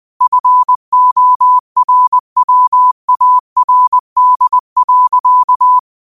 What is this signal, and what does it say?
Morse code, heard as a keyed tone, about 20 wpm, 1000 Hz. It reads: FORWARD.
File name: qotdfiles / 20wpm